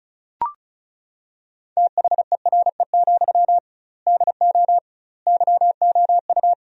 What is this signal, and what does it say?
Morse code, heard as a keyed tone, about 34 wpm, 700 Hz. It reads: THERE, DO YOU
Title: 2x Word-Spacing - Sentences from Top 300 Words - 35wpm